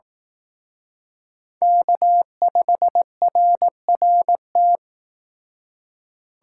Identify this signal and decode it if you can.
Morse code, heard as a keyed tone, about 18 wpm, 700 Hz. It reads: K5RRT
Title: Speed Racing - USA 1x3 Call Signs 15wpm